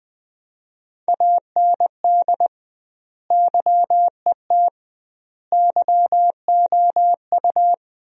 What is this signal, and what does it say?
Morse code, heard as a keyed tone, about 20 wpm, 700 Hz. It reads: AND YET YOU
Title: 2x Word-Spacing - Sentences from Top 500 Words - 20wpm